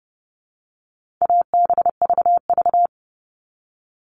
Morse code, heard as a keyed tone, about 30 wpm, 700 Hz. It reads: ABVV
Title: All Cut Numbers - 4-Digit 30wpm